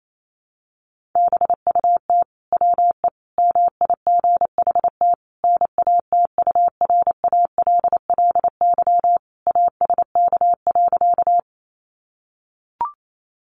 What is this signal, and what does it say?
Morse code, heard as a keyed tone, about 28 wpm, 700 Hz. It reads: BUT WE MIGHT NATURALLY ASK.
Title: Sentences from Top 1300 Words 28wpm